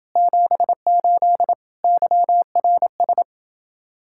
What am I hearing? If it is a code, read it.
Morse code, heard as a keyed tone, about 28 wpm, 700 Hz. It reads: 78 YRS